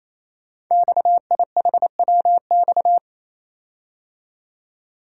Morse code, heard as a keyed tone, about 29 wpm, 700 Hz. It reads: XIHWX
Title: Code Groups - 5 Letters - 28wpm